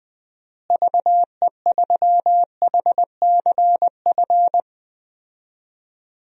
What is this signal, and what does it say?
Morse code, heard as a keyed tone, about 20 wpm, 700 Hz. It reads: VE3HCF
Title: DXCC-100 Call Signs - 6 Characters - Repeated 3x - 20wpm